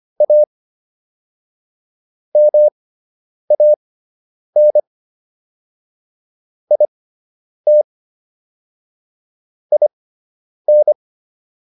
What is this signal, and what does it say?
Morse code, heard as a keyed tone, about 25 wpm, spaced as for 9 wpm, 600 Hz. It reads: A MAN IT IN